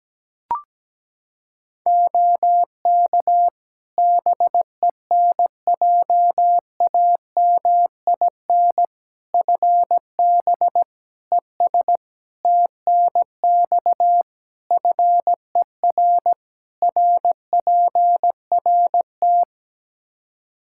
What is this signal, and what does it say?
Morse code, heard as a keyed tone, about 17 wpm, 700 Hz. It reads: OK BENJAMIN FB ES TNX FER RPRT
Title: K1USN SST QSO Practice 17wpm